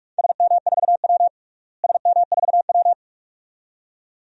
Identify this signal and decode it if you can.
Morse code, heard as a keyed tone, about 45 wpm, 700 Hz. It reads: SM4W SM4W
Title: DXCC-100 Call Signs - 4 Characters - Repeated 2x - 45wpm